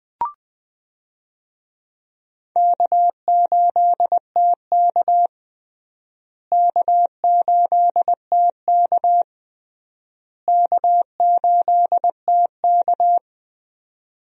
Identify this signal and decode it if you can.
Morse code, heard as a keyed tone, about 20 wpm, 700 Hz. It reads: K8TK K8TK K8TK
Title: USA Call Signs - 4 Characters - Repeated 3x - 20wpm